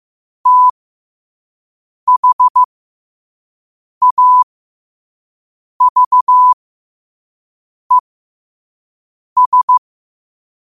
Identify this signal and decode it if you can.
Morse code, heard as a keyed tone, about 15 wpm, spaced as for 5 wpm, 1000 Hz. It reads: THAVES
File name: qotdfiles / 5wpm